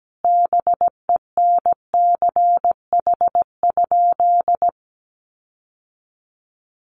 Morse code, heard as a keyed tone, about 17 wpm, 700 Hz. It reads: BENCH?
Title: Sentences from Top 1400 Words 17wpm